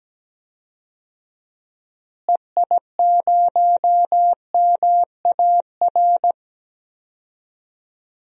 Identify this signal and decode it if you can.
Morse code, heard as a keyed tone, about 17 wpm, 700 Hz. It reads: EI0MAR
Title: DXCC-100 Call Signs - 6 Characters - Repeated 3x - 17wpm